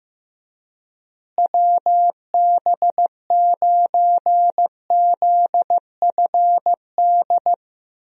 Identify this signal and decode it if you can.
Morse code, heard as a keyed tone, about 15 wpm, 700 Hz. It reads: WB9ZFD
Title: USA Call Signs - 6 Characters - Repeated 2x - 15wpm